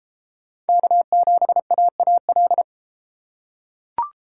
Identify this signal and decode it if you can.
Morse code, heard as a keyed tone, about 29 wpm, 700 Hz. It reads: K7AAL